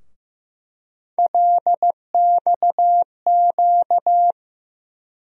Morse code, heard as a keyed tone, about 15 wpm, 700 Hz. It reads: LXQ
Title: USA Call Sign Postfixes - 3 Letters 15wpm